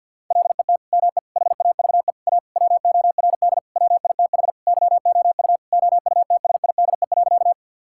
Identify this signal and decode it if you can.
Morse code, heard as a keyed tone, about 50 wpm, 700 Hz. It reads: LET ME HAVE A WORD WITH YOU OUTSIDE.
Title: Sentences from Top 1000 Words - Review Part 6 - 50wpm